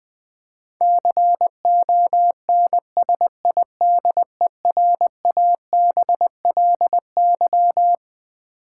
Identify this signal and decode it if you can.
Morse code, heard as a keyed tone, about 20 wpm, 700 Hz. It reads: CONSIDERABLY
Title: Twelve Letter Words - 20wpm